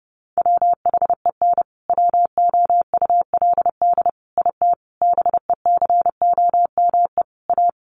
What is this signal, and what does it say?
Morse code, heard as a keyed tone, about 30 wpm, 700 Hz. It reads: WHEN WOULD IT BECOME A
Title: Sentences from Top 1200 Words 30wpm